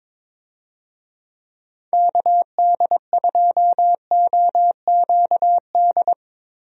Morse code, heard as a keyed tone, about 22 wpm, 700 Hz. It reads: KD2OQD
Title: USA Call Signs - 6 Characters - Repeated 3x - 22wpm